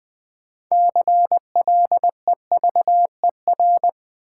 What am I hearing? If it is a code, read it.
Morse code, heard as a keyed tone, about 20 wpm, 700 Hz. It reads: CLEVER
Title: 6 Letter Words - First 3 Letters Given - 20wpm